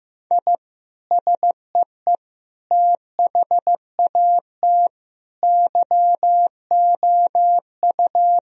Morse code, heard as a keyed tone, about 15 wpm, 700 Hz. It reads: I SEE THAT YOU
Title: Sentences from Top 1000 Words - Review Part 3 - 15wpm